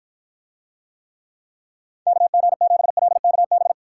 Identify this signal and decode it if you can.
Morse code, heard as a keyed tone, about 48 wpm, 700 Hz. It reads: KG7LKB